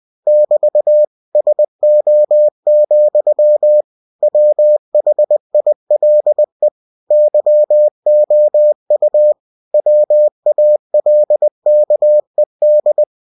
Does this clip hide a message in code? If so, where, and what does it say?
Yes: Morse code from 0.3 to 13.0 s, heard as a keyed tone, about 20 wpm, 600 Hz. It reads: <BT> SO, WHILE YOU WALKED